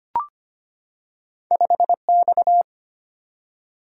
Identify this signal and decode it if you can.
Morse code, heard as a keyed tone, about 25 wpm, 700 Hz. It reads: 5X
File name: Two Letter-Number Combinations - 25wpm